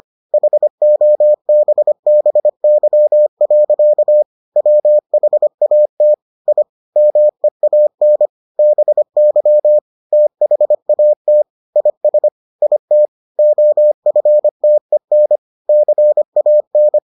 Morse code, heard as a keyed tone, about 25 wpm, 600 Hz. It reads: HOBBY. WHAT I MEAN BY THAT IS IT OFTEN CAN